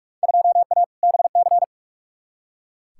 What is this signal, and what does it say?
Morse code, heard as a keyed tone, about 45 wpm, 700 Hz. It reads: 2A BC
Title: Field Day Exchanges - 45wpm